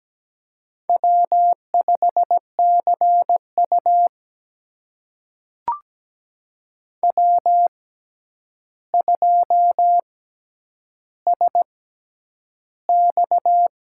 Morse code, heard as a keyed tone, about 17 wpm, spaced as for 9 wpm, 700 Hz. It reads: W5CU W2SX
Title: USA Call Signs - 4 Characters - 6x Character Spacing - 17wpm